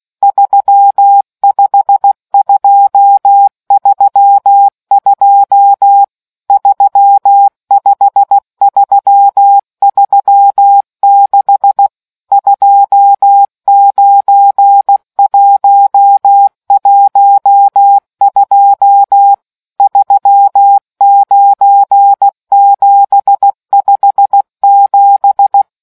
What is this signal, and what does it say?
Morse code, heard as a keyed tone, about 16 wpm, 800 Hz. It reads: 35232 35336 29112 39757